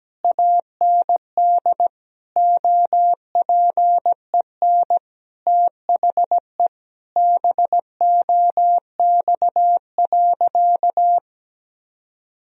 Morse code, heard as a keyed tone, about 17 wpm, 700 Hz. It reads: AND OPEN THE BOX.
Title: Sentences from Top 1000 Words - Review Part 4 - 17wpm